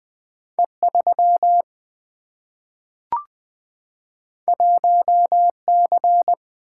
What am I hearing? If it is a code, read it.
Morse code, heard as a keyed tone, about 20 wpm, 700 Hz. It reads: E 3 1 C
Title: Single-Letter Single-Number 20wpm